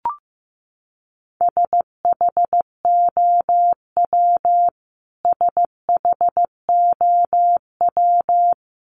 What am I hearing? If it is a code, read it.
Morse code, heard as a keyed tone, about 15 wpm, 700 Hz. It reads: SHOW SHOW
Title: Top 300 Words Repeated 15wpm